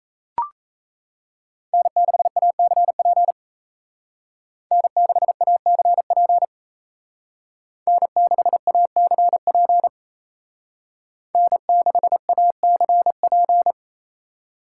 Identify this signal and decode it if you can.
Morse code, heard as a keyed tone, about 34 wpm, 700 Hz. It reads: N6ACP N6ACP N6ACP N6ACP
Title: Speed Racing - USA 1x3 Call Signs 28wpm